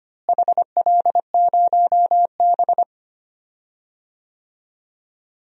Morse code, heard as a keyed tone, about 25 wpm, 700 Hz. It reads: HL0B
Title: DXCC-100 Call Signs - 4 Characters - Repeated 2x - 25wpm